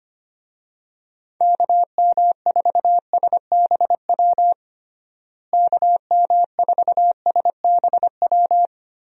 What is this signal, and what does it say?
Morse code, heard as a keyed tone, about 25 wpm, 700 Hz. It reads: KM4SBW KM4SBW
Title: USA Call Signs - 6 Characters - Repeated 3x - 25wpm